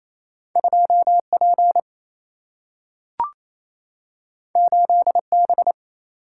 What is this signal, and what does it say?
Morse code, heard as a keyed tone, about 28 wpm, 700 Hz. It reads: 2 P 8 B
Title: Single-Letter Single-Number 28wpm